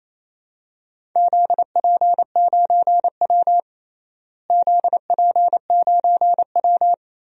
Morse code, heard as a keyed tone, about 27 wpm, 700 Hz. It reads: ZP9W ZP9W
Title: DXCC-100 Call Signs - 4 Characters - Repeated 3x - 28wpm